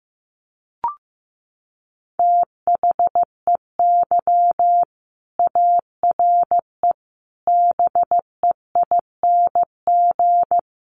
Morse code, heard as a keyed tone, about 15 wpm, 700 Hz. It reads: THEY ARE BEING
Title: Sentences from Top 600 Words 15wpm